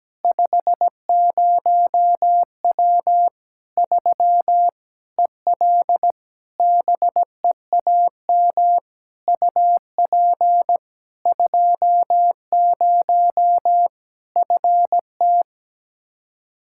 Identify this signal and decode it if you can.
Morse code, heard as a keyed tone, about 17 wpm, 700 Hz. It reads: 50W 3 EL BEAM UP 20 FT
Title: Practice QSO - Exchange 2 - Info - 17wpm